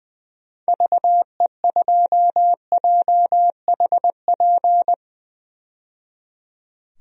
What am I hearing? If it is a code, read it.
Morse code, heard as a keyed tone, about 20 wpm, 700 Hz. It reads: VE2JHP
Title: DXCC-100 Call Signs - 6 Characters - 20wpm